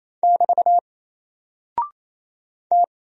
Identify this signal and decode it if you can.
Morse code, heard as a keyed tone, about 28 wpm, 700 Hz. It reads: <BT> T
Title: Single Character (Letter:Number:Punctuation:Prosign) Alt 28wpm